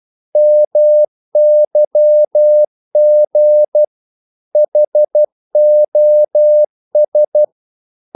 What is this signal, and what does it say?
Morse code, heard as a keyed tone, about 12 wpm, 600 Hz. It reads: MYG HOS
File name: Korte ord DK 12wpm